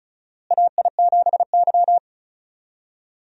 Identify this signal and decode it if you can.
Morse code, heard as a keyed tone, about 36 wpm, 700 Hz. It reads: AI7Y
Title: USA Call Signs - 4 Characters - Repeated 3x - 35wpm